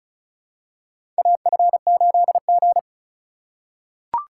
Morse code, heard as a keyed tone, about 36 wpm, 700 Hz. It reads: AF8G
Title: USA Call Signs - 4 Characters - 3x Character Spacing - 35wpm